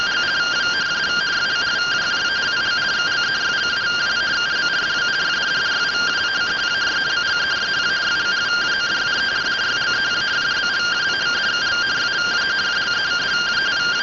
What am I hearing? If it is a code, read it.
Two-tone teleprinter signal, about 170 Hz shift.